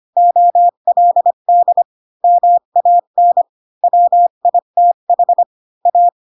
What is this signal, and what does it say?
Morse code, heard as a keyed tone, about 25 wpm, 700 Hz. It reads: OLD MAN WITH A
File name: Pinocchio / Pinocchio ch7 20 25 ik1hge